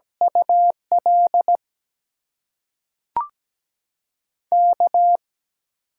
Morse code, heard as a keyed tone, about 17 wpm, spaced as for 5 wpm, 700 Hz. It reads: ULK
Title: USA Call Signs - 6 Characters - 8x Character Spacing - 17wpm